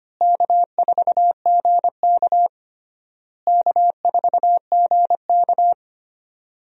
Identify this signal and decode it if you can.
Morse code, heard as a keyed tone, about 25 wpm, 700 Hz. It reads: K4GK K4GK
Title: USA Call Signs - 4 Characters - Repeated 3x - 25wpm